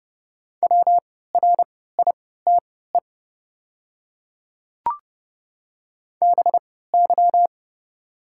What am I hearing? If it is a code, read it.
Morse code, heard as a keyed tone, about 30 wpm, spaced as for 17 wpm, 700 Hz. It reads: WRITE BY
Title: 3x Character Spacing - Top 500 Words - 30wpm